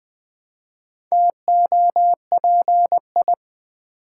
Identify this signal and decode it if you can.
Morse code, heard as a keyed tone, about 20 wpm, 700 Hz. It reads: TOPI
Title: Build A Word - 1 Letter At A Time - 3 to 5-Letter Words - 20wpm